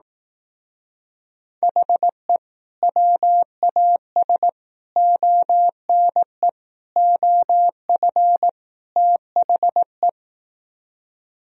Morse code, heard as a keyed tone, about 18 wpm, 700 Hz. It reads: HE WAS ONE OF THE
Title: Speed-Racing - Sets of 5 Words 15wpm